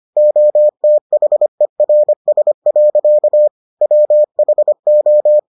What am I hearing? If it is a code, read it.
Morse code, heard as a keyed tone, about 25 wpm, 600 Hz. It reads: OTHERS. WHO